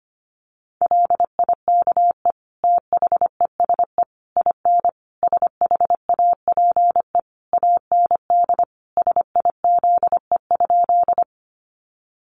Morse code, heard as a keyed tone, about 25 wpm, 700 Hz. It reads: LIKE THESE IN SHAPE AND SIZE?
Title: Sentences from Top 500 Words 25wpm